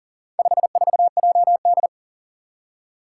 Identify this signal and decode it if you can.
Morse code, heard as a keyed tone, about 40 wpm, 700 Hz. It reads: 54JD
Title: Four Challenging Letter-Number Combinations 40wpm